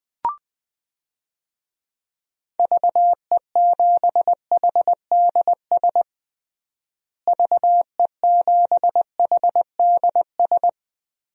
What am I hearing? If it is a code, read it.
Morse code, heard as a keyed tone, about 20 wpm, 700 Hz. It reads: VE7HDS VE7HDS
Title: DXCC-100 Call Signs - 6 Characters - Repeated 2x - 20wpm